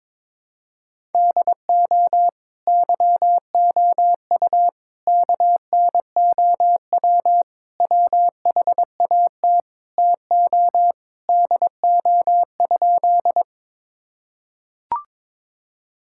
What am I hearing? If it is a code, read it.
Morse code, heard as a keyed tone, about 22 wpm, 700 Hz. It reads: DO YOU KNOW WHAT TO DO?
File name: practice / 22wpm / Sentences with 2-Word Phrases - 22wpm